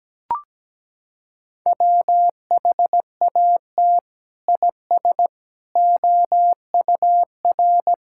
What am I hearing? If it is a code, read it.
Morse code, heard as a keyed tone, about 17 wpm, 700 Hz. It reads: WHAT IS OUR
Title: Sentences from Top 1000 Words - Review Part 6 - 17wpm